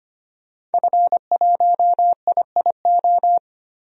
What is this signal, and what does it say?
Morse code, heard as a keyed tone, about 25 wpm, 700 Hz. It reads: F1IIO
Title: DXCC-100 Call Signs - 5 Characters - 25wpm